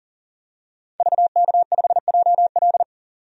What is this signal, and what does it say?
Morse code, heard as a keyed tone, about 40 wpm, 700 Hz. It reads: VK5JL
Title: DXCC-100 Call Signs - 5 Characters - 40wpm